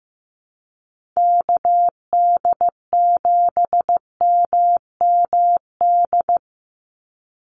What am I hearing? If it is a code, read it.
Morse code, heard as a keyed tone, about 15 wpm, 700 Hz. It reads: KD7MMD
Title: USA Call Signs 15wpm